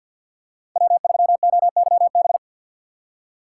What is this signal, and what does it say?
Morse code, heard as a keyed tone, about 52 wpm, 700 Hz. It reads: W3OYB
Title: USA Call Signs - 5 Characters - 2x Character Spacing - 50wpm